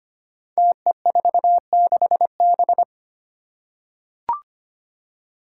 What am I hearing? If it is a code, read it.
Morse code, heard as a keyed tone, about 25 wpm, 700 Hz. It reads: TE46B
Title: Five Challenging Letter-Number Combinations 25wpm